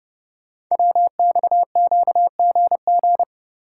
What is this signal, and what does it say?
Morse code, heard as a keyed tone, about 30 wpm, 700 Hz. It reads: WXQGG
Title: Code Groups - 5 Letters - 30wpm